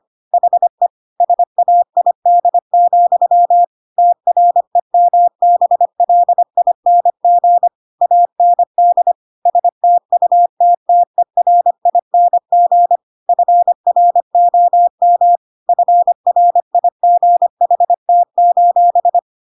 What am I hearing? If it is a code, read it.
Morse code, heard as a keyed tone, about 25 wpm, 700 Hz. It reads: HE SAID, TREMBLING AND STUTTERING FROM FRIGHT:
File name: Pinocchio / Pinocchio ch1 25 25 ik1hge